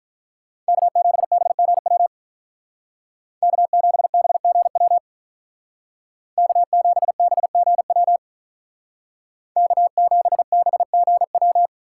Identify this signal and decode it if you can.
Morse code, heard as a keyed tone, about 45 wpm, 700 Hz. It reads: K7BGW K7BGW K7BGW K7BGW